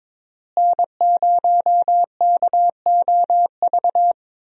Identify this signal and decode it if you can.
Morse code, heard as a keyed tone, about 22 wpm, 700 Hz. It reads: N0KOV
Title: USA Call Signs - 5 Characters - Repeated 3x - 22wpm